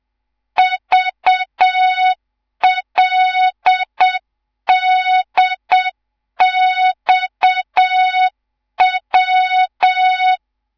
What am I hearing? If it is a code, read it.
Morse code, heard as a keyed tone, about 7 wpm, 3700 Hz. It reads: VLDXW
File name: Morze-HA2MN/hang/11 GYAKORLAT